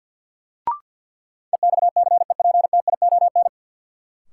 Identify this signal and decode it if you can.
Morse code, heard as a keyed tone, about 53 wpm, 700 Hz. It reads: EXCEPTION
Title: Sets of 4 Words - New Word Drill - 50wpm